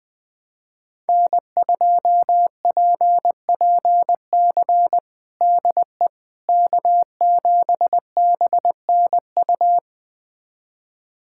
Morse code, heard as a keyed tone, about 20 wpm, 700 Hz. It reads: N2PPC DE K7BNU
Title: USA Callsign DE Callsign - 5 Characters - 20wpm